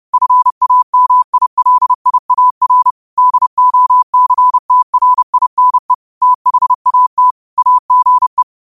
Morse code, heard as a keyed tone, about 30 wpm, 1000 Hz. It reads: FAMILIAR DOCTRINE THAT AGE